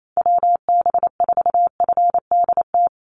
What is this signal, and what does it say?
Morse code, heard as a keyed tone, about 28 wpm, 700 Hz. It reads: WB4FDT
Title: Call Signs Repeated 3x - 28wpm